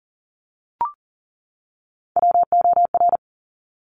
Morse code, heard as a keyed tone, about 42 wpm, 700 Hz. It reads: WOR
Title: Top Three Letter Combinations in Words 40wpm